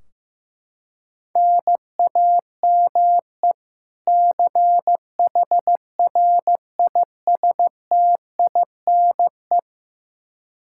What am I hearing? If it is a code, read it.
Morse code, heard as a keyed tone, about 15 wpm, 700 Hz. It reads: NAME CHRISTINE
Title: Practice QSO - Exchange 1 - Beginning - 15wpm